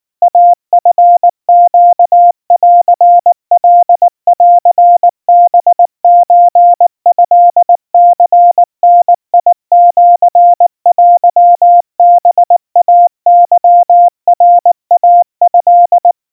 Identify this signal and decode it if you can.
Morse code (Japanese Wabun code), heard as a keyed tone, about 19 wpm, 700 Hz. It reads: イチネンカンハソトニダシテハイケナイト
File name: kawa / Morse sample / kamyu 70 cpm 700 Hz Crispy